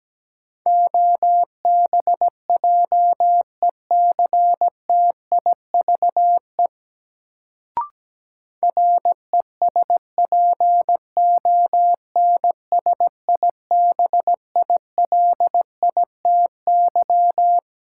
Morse code, heard as a keyed tone, about 17 wpm, 700 Hz. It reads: OBJECTIVE RESPONSIBILITY
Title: Sentences from Top 1200 Words - New Word Drill - 17wpm